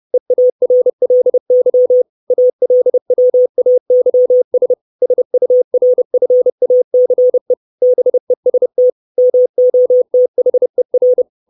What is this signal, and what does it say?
Morse code, heard as a keyed tone, about 30 wpm, 500 Hz. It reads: EARLY ALWAYS SURFACE BEST MOTHER
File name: 632 Words 30WPM